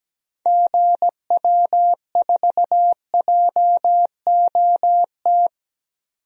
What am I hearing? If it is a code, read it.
Morse code, heard as a keyed tone, about 17 wpm, 700 Hz. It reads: GW4JOT
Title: DXCC-100 Call Signs - 6 Characters - 17wpm